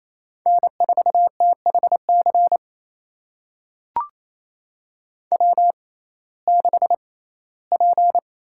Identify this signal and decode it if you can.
Morse code, heard as a keyed tone, about 28 wpm, spaced as for 17 wpm, 700 Hz. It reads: N4THC W6P
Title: USA Call Signs - 5 Characters - 6x Character Spacing - 28wpm